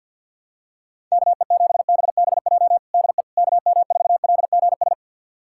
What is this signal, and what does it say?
Morse code, heard as a keyed tone, about 50 wpm, 700 Hz. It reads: KE7BBJ DE KM4LGS
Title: USA Callsign DE Callsign 50wpm